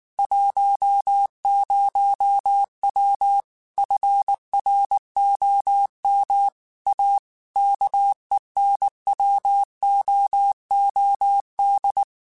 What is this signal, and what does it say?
Morse code, heard as a keyed tone, about 19 wpm, 800 Hz. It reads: 10W FROM A KENWOOD